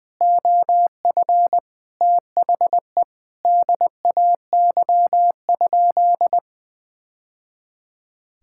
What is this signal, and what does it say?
Morse code, heard as a keyed tone, about 20 wpm, 700 Hz. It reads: OF THE DAY?
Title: Sentences from Top 1000 Words - Review Part 6 - 20wpm